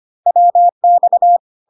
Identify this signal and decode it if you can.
Morse code, heard as a keyed tone, about 25 wpm, 700 Hz. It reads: WX
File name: CW abbreviations alphabetical repeated 3 times 25 25wpm